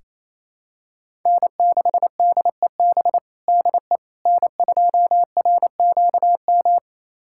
Morse code, heard as a keyed tone, about 27 wpm, 700 Hz. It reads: N6DEB DE N2RQM
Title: USA Callsign DE Callsign - 5 Characters - 28wpm